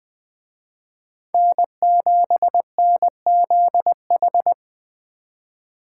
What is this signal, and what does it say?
Morse code, heard as a keyed tone, about 24 wpm, 700 Hz. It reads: N7NZH
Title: Speed Racing - USA Call Signs 20wpm